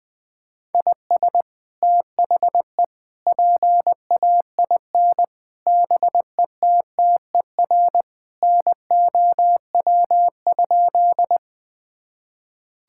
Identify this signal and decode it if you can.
Morse code, heard as a keyed tone, about 20 wpm, 700 Hz. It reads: IS THE PAIN BETTER NOW?
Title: Sentences from Top 1600 Words 20wpm